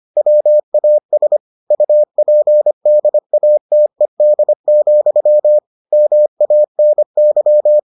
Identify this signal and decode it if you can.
Morse code, heard as a keyed tone, about 25 wpm, 600 Hz. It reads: WAS UPDATED, MANY